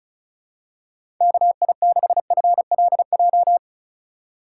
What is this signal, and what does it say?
Morse code, heard as a keyed tone, about 36 wpm, 700 Hz. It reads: KI6FLJ
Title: USA Call Signs - 6 Characters - 3x Character Spacing - 35wpm